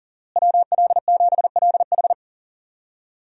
Morse code, heard as a keyed tone, about 40 wpm, 700 Hz. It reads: WL7LH
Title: USA Call Signs - 5 Characters - Repeated 2x - 40wpm